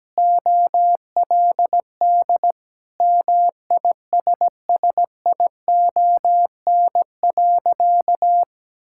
Morse code, heard as a keyed tone, about 17 wpm, 700 Hz. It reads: OLD MISSION.
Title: Sentences from Top 1600 Words 17wpm